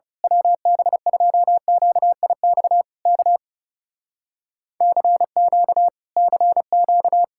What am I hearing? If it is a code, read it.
Morse code, heard as a keyed tone, about 28 wpm, 700 Hz. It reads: WB2QIX K CQ CQ
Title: Speed-Racing Common QSO Phrases - 30wpm